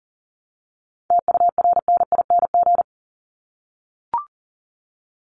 Morse code, heard as a keyed tone, about 40 wpm, 700 Hz. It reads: TURNING
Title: ING Suffix - Top 100 Words with Base Words 40wpm